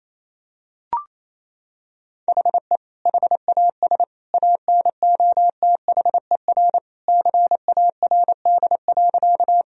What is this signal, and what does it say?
Morse code, heard as a keyed tone, about 28 wpm, 700 Hz. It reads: HE HAS ANOTHER CARD.
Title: Sentences from Top 1000 Words - Review Part 4 - 28wpm